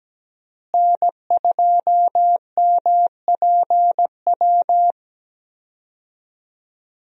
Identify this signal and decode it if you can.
Morse code, heard as a keyed tone, about 17 wpm, 700 Hz. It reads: N2MPW
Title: USA Call Signs - 5 Characters - Repeated 3x - 17wpm